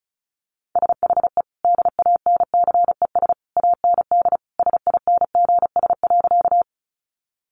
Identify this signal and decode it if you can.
Morse code, heard as a keyed tone, about 34 wpm, 700 Hz. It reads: SHE DANCES AND SINGS.
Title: Easy Phrases 35wpm